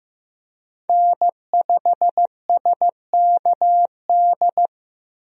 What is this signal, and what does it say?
Morse code, heard as a keyed tone, about 15 wpm, 700 Hz. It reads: N5SKD
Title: USA Call Signs - 5 Characters - Repeated 2x - 15wpm